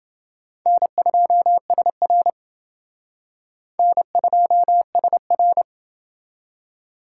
Morse code, heard as a keyed tone, about 28 wpm, 700 Hz. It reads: N2SR N2SR